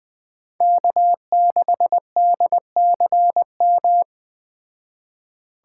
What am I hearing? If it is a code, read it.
Morse code, heard as a keyed tone, about 20 wpm, 700 Hz. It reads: K6DCM
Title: Speed Racing - USA 1x3 Call Signs 17wpm